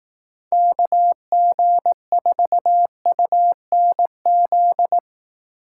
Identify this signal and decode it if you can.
Morse code, heard as a keyed tone, about 18 wpm, 700 Hz. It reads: KG4UNZ
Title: Speed Racing - USA Call Signs 15wpm